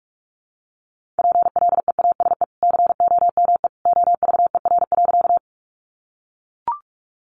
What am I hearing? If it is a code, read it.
Morse code, heard as a keyed tone, about 44 wpm, 700 Hz. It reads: PLEASE COME OVER.
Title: Sentences from Top 1500 Words 45wpm